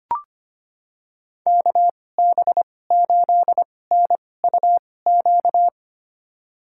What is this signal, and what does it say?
Morse code, heard as a keyed tone, about 25 wpm, spaced as for 18 wpm, 700 Hz. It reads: KB8NUQ
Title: USA Call Signs - 6 Characters - 2x Character Spacing - 25wpm